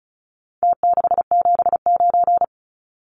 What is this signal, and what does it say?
Morse code, heard as a keyed tone, about 34 wpm, 700 Hz. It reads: T679
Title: Numbers - Contest Style - 4-Digit 35wpm